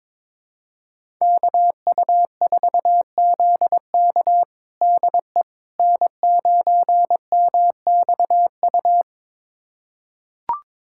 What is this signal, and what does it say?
Morse code, heard as a keyed tone, about 22 wpm, 700 Hz. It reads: KU4ZK DE N9MXU
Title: USA Callsign DE Callsign - 5 Characters - 22wpm